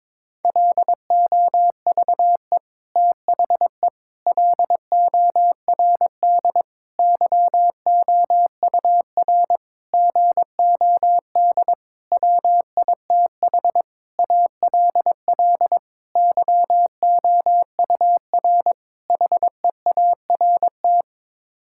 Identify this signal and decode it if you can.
Morse code, heard as a keyed tone, about 22 wpm, 700 Hz. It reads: LOVE THE LORD YOUR GOD WITH ALL YOUR HEART
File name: Sets of 9 Words - 22wpm